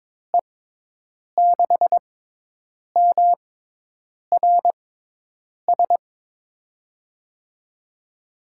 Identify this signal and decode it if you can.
Morse code, heard as a keyed tone, about 22 wpm, spaced as for 8 wpm, 700 Hz. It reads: E6MRS
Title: USA Call Signs - 6 Characters - 6x Character Spacing - 22wpm